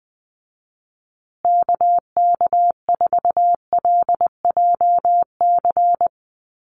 Morse code, heard as a keyed tone, about 20 wpm, 700 Hz. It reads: KK4LJC
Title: USA Call Signs Repeated 3x 20wpm